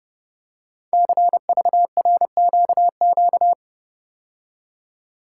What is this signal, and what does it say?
Morse code, heard as a keyed tone, about 29 wpm, 700 Hz. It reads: CVRQQ
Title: Code Groups - 5 Letters - 30wpm